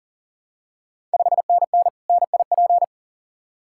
Morse code, heard as a keyed tone, about 40 wpm, 690 Hz. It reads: 5NN NIP
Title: Ontario QSO Party Exchange 40wpm